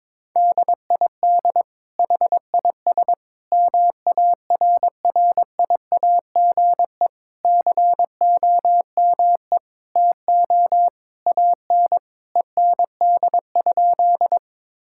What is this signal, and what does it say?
Morse code, heard as a keyed tone, about 22 wpm, 700 Hz. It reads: DID HIS MARRIAGE COME TO AN END?
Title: Sentences from Top 1000 Words - Review Part 5 - 22wpm